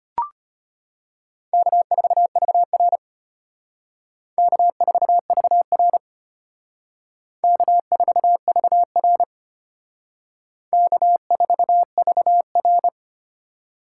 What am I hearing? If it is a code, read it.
Morse code, heard as a keyed tone, about 32 wpm, 700 Hz. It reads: K4VR K4VR K4VR K4VR